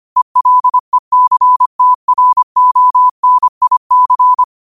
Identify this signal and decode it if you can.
Morse code, heard as a keyed tone, about 25 wpm, 1000 Hz. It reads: ELECTRONIC
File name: qotdfiles / 25wpm